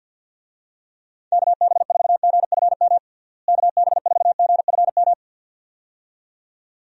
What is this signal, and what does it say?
Morse code, heard as a keyed tone, about 50 wpm, 700 Hz. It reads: KB4GFM KB4GFM
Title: USA Call Signs - 6 Characters - Repeated 2x - 50wpm